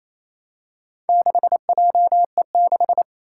Morse code, heard as a keyed tone, about 28 wpm, 700 Hz. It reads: 6JE6
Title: Four Challenging Letter-Number Combinations 28wpm